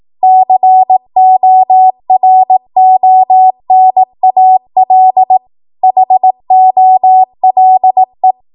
Morse code, heard as a keyed tone, about 18 wpm, 750 Hz. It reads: CORONAL HOLE
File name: Morse / 18WPM